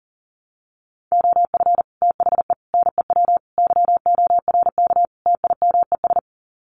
Morse code, heard as a keyed tone, about 40 wpm, 700 Hz. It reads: OF THE NEW YORK TIMES
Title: Sets of 5 Words 40wpm